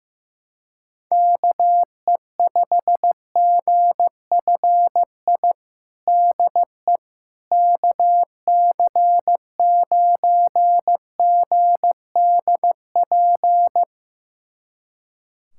Morse code, heard as a keyed tone, about 15 wpm, 700 Hz. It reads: KE5GFI DE KC9GDP
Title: USA Callsign DE Callsign 15wpm